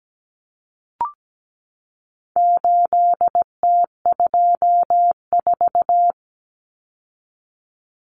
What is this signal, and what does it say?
Morse code, heard as a keyed tone, about 17 wpm, 700 Hz. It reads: 8T24
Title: Common Cut Numbers - 4-Digit 17wpm